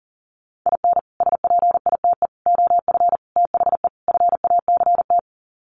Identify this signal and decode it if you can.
Morse code, heard as a keyed tone, about 40 wpm, 700 Hz. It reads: IN SPITE OF THE FACT
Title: Sets of 5 Words Encore2 40wpm